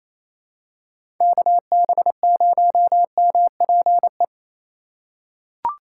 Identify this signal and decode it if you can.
Morse code, heard as a keyed tone, about 28 wpm, 700 Hz. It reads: KB0MPE
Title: USA Call Signs - 6 Characters - 4x Character Spacing - 28wpm